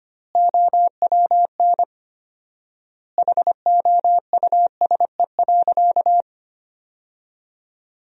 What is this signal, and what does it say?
Morse code, heard as a keyed tone, about 25 wpm, 700 Hz. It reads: OWN HOUSE.
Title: 4x Word-Spacing - Sentences from Top 300 Words - 25wpm